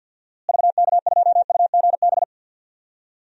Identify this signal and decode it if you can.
Morse code, heard as a keyed tone, about 49 wpm, 700 Hz. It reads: VK2UGB
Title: DXCC-100 Call Signs - 6 Characters - Repeated 3x - 50wpm